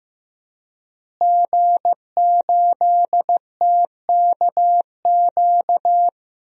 Morse code, heard as a keyed tone, about 15 wpm, 700 Hz. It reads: G8TKQ
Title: DXCC-100 Call Signs - 5 Characters - Repeated 2x - 15wpm